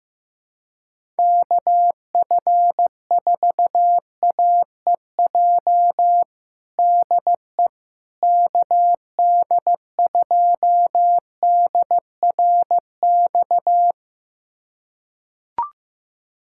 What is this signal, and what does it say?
Morse code, heard as a keyed tone, about 15 wpm, 700 Hz. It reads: KF4AEJ DE KD2DRX
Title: USA Callsign DE Callsign - 6 Characters - 15wpm